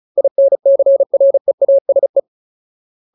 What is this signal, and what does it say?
Morse code, heard as a keyed tone, about 36 wpm, 550 Hz. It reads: INCREASE
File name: Ultimate Word List / 35WPM Ultimate